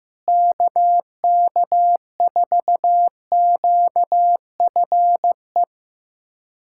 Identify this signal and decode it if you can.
Morse code, heard as a keyed tone, about 15 wpm, 700 Hz. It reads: KK4QFE
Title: USA Callsign DE Callsign - 6 Characters - 15wpm